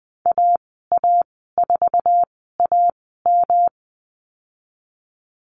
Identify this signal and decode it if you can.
Morse code, heard as a keyed tone, about 20 wpm, spaced as for 14 wpm, 700 Hz. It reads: AA4AM
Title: 122 Lesson 20g - Number 3 - Call Signs 20wpm